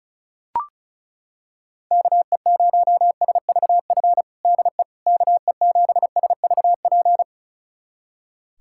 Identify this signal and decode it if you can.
Morse code, heard as a keyed tone, about 34 wpm, 700 Hz. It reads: KE0SVF DE KE7SVP
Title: USA Callsign DE Callsign - 6 Characters - 35wpm